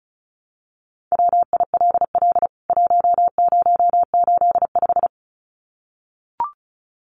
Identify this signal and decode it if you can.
Morse code, heard as a keyed tone, about 34 wpm, 700 Hz. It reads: WILL 1085
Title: CWT Exchange 35wpm